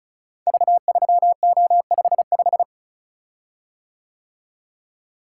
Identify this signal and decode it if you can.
Morse code, heard as a keyed tone, about 36 wpm, 700 Hz. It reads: V3O55
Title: Code Groups - 5 Letter:Numbers - 35wpm